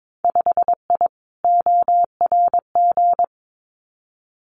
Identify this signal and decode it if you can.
Morse code, heard as a keyed tone, about 22 wpm, 700 Hz. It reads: 5I ORG
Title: Winter Field Day Exchanges Encore 22wpm